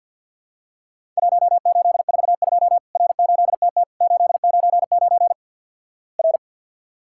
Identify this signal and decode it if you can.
Morse code, heard as a keyed tone, about 50 wpm, 700 Hz. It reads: 1842 R8TT 899 R
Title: Worked All Europe DX Contest - QTC Exchange - 50wpm